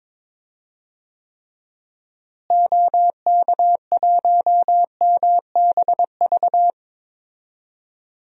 Morse code, heard as a keyed tone, about 22 wpm, 700 Hz. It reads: OK1MBV
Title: DXCC-100 Call Signs - 6 Characters - Repeated 3x - 22wpm